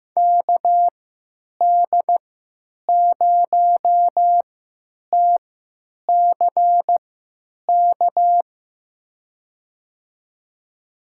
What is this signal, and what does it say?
Morse code, heard as a keyed tone, about 15 wpm, spaced as for 9 wpm, 700 Hz. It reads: KD0TCK
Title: USA Call Signs - 3x Character Spacing - 15wpm